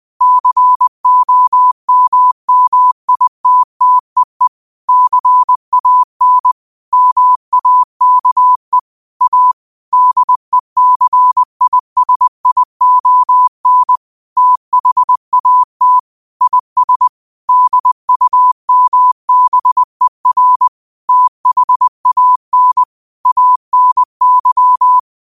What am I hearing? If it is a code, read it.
Morse code, heard as a keyed tone, about 20 wpm, 1000 Hz. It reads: COMMITTEE CAN MAKE A DECISION THAT IS DUMBER THAN ANY
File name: qotdfiles / 20wpm